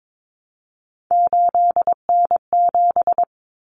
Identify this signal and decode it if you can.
Morse code, heard as a keyed tone, about 22 wpm, 700 Hz. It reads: 8N7
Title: Common Cut Numbers - 3-Digit 22wpm